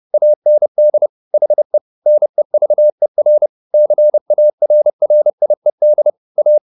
Morse code, heard as a keyed tone, about 30 wpm, 600 Hz. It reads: AND HE NEVER CARRIED A
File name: the old man and the sea / the old man and the sea 1 30wpm